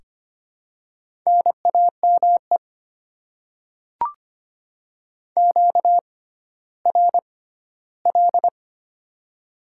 Morse code, heard as a keyed tone, about 25 wpm, spaced as for 14 wpm, 700 Hz. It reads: NAME QRL
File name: Top 20 QSO Elements - 6x Character Spacing - 25wpm